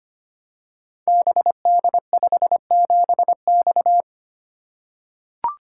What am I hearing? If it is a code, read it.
Morse code, heard as a keyed tone, about 25 wpm, 700 Hz. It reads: BD57X
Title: Code Groups - 5 Letter:Numbers - 25wpm